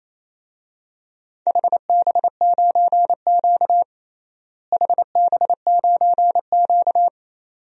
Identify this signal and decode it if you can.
Morse code, heard as a keyed tone, about 28 wpm, 700 Hz. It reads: HB9Q HB9Q
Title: DXCC-100 Call Signs - 4 Characters - Repeated 2x - 28wpm